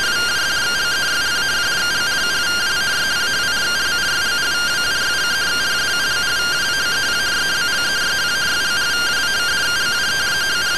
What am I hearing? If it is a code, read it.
Two-tone teleprinter signal, about 170 Hz shift.